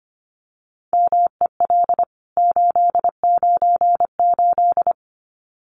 Morse code, heard as a keyed tone, about 25 wpm, 700 Hz. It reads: MEL 898
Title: CWT Exchange 25wpm